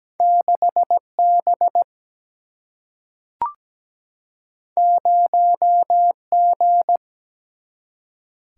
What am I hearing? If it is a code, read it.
Morse code, heard as a keyed tone, about 17 wpm, 700 Hz. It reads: 6B 0G
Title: Single-Letter Single-Number 17wpm